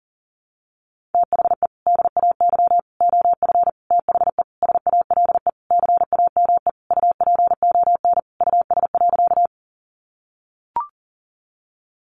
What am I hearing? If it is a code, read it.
Morse code, heard as a keyed tone, about 40 wpm, 700 Hz. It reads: THE DAY OF THE SALE CAME UPON US.
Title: Sentences from Top 1400 Words 40wpm